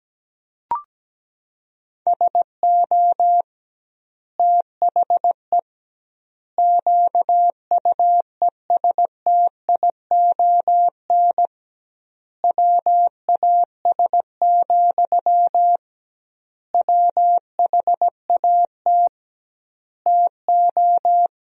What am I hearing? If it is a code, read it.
Morse code, heard as a keyed tone, about 17 wpm, 700 Hz. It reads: SO THE QUESTION WAS, WHAT TO
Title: 2x Word-Spacing - Sentences from Top 400 Words - 17wpm